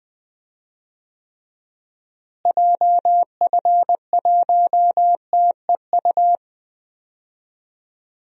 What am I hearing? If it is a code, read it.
Morse code, heard as a keyed tone, about 21 wpm, 700 Hz. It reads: JF1TEU
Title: DXCC-100 Call Signs - 6 Characters - Repeated 2x - 20wpm